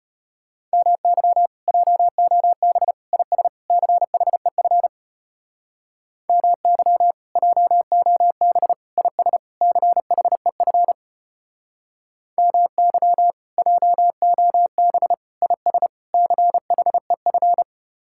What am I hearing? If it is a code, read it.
Morse code, heard as a keyed tone, about 33 wpm, 700 Hz. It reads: MY JOB IS CHEF MY JOB IS CHEF MY JOB IS CHEF